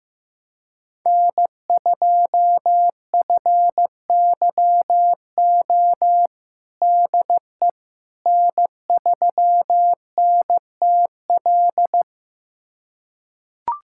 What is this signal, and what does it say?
Morse code, heard as a keyed tone, about 15 wpm, 700 Hz. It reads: N2FYO DE N3NTL
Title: USA Callsign DE Callsign - 5 Characters - 15wpm